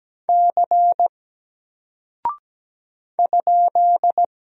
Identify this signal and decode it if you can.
Morse code, heard as a keyed tone, about 17 wpm, 700 Hz. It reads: C ?
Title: Single Character (Letter:Number:Punctuation:Prosign) 17wpm